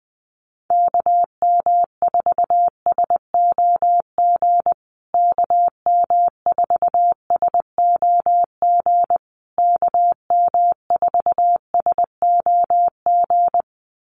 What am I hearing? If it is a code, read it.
Morse code, heard as a keyed tone, about 20 wpm, 700 Hz. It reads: KM4SOG KM4SOG KM4SOG
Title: USA Call Signs Repeated 3x 20wpm